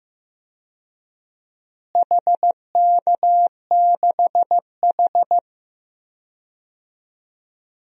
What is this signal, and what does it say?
Morse code, heard as a keyed tone, about 15 wpm, 700 Hz. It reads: HK6H
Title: DXCC-100 Call Signs - 4 Characters - Repeated 2x - 15wpm